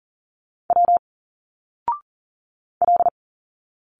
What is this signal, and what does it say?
Morse code, heard as a keyed tone, about 39 wpm, 700 Hz. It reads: W L
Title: Single Letters 40wpm